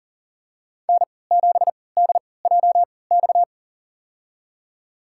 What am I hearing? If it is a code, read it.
Morse code, heard as a keyed tone, about 40 wpm, spaced as for 23 wpm, 700 Hz. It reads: N7DJX
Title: USA Call Signs - 5 Characters - 3x Character Spacing - 40wpm